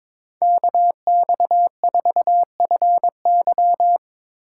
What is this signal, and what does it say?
Morse code, heard as a keyed tone, about 21 wpm, 700 Hz. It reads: KX4FY
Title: USA Call Signs - 5 Characters - 2x Character Spacing - 22wpm